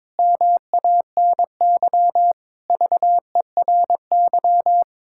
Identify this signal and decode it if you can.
Morse code, heard as a keyed tone, about 22 wpm, 700 Hz. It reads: MANY VERY
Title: Pangram Challenge v2 - 22wpm